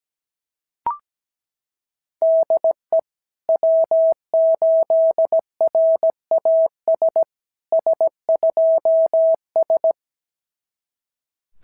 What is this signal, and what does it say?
Morse code, heard as a keyed tone, about 17 wpm, 650 Hz. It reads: DE W8RAS S2S
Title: SOTA QSO Practice 17wpm